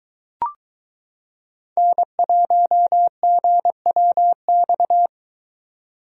Morse code, heard as a keyed tone, about 22 wpm, 700 Hz. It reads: N1GWX